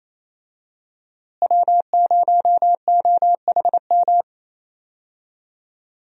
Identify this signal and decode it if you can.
Morse code, heard as a keyed tone, about 28 wpm, 700 Hz. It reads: W0OHM
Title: USA Call Signs - 5 Characters - Repeated 2x - 28wpm